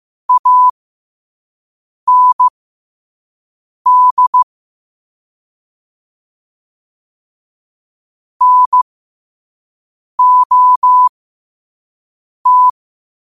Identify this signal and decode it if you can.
Morse code, heard as a keyed tone, about 15 wpm, spaced as for 5 wpm, 1000 Hz. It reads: AND NOT
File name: qotdfiles / 5wpm